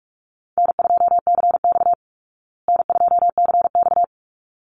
Morse code, heard as a keyed tone, about 44 wpm, 700 Hz. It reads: N2CX N2CX
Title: Easy Call Signs Repeated 2x - 45wpm